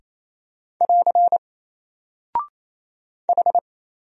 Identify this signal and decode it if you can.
Morse code, heard as a keyed tone, about 27 wpm, 700 Hz. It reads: <AR> H
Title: Single Character (Letter:Number:Punctuation:Prosign) 28wpm